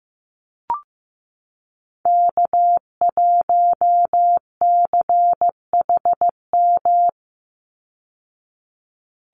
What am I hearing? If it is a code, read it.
Morse code, heard as a keyed tone, about 15 wpm, 700 Hz. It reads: K1CHM
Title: Call Signs Repeated 2x - 15wpm